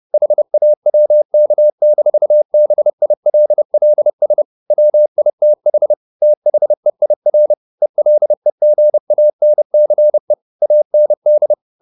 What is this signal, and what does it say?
Morse code, heard as a keyed tone, about 30 wpm, 600 Hz. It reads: HAWK-BILLS WITH THEIR ELEGANCE AND
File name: the old man and the sea / the old man and the sea 1 30wpm